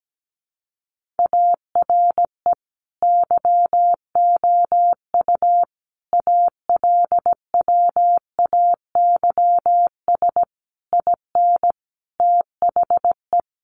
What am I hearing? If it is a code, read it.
Morse code, heard as a keyed tone, about 17 wpm, 700 Hz. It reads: ARE YOU ALWAYS IN THE
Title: Sentences from Top 1400 Words 17wpm